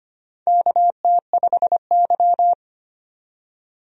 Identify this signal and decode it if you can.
Morse code, heard as a keyed tone, about 25 wpm, 700 Hz. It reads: KT5Y
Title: USA Call Signs - 4 Characters - 25wpm